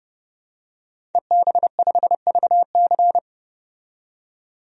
Morse code, heard as a keyed tone, about 30 wpm, 700 Hz. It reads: EB5VC
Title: DXCC-100 Call Signs - 5 Characters - 30wpm